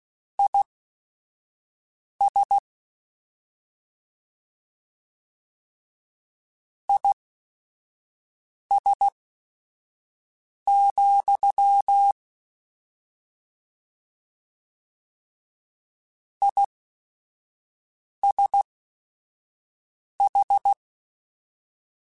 Morse code, heard as a keyed tone, about 16 wpm, spaced as for 5 wpm, 800 Hz. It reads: IS IS, ISH